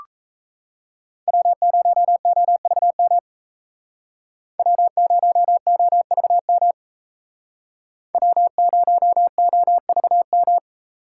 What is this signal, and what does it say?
Morse code, heard as a keyed tone, about 36 wpm, 700 Hz. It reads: W0OVM W0OVM W0OVM